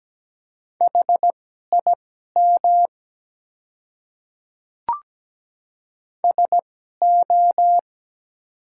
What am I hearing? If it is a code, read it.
Morse code, heard as a keyed tone, about 17 wpm, spaced as for 12 wpm, 700 Hz. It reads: HIM SO
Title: 2x Character Spacing - Top 100 Words - 17wpm